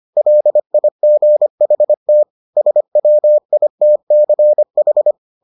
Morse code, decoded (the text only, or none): LIGHT SWITCH